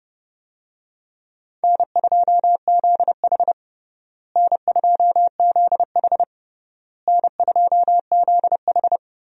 Morse code, heard as a keyed tone, about 30 wpm, 700 Hz. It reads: N2ZH N2ZH N2ZH
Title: USA Call Signs - 4 Characters - Repeated 3x - 30wpm